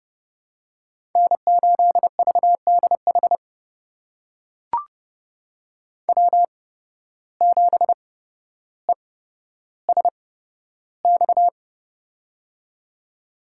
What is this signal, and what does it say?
Morse code, heard as a keyed tone, about 30 wpm, spaced as for 13 wpm, 700 Hz. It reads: N8VDH W7ESX
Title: USA Call Signs - 5 Characters - 8x Character Spacing - 30wpm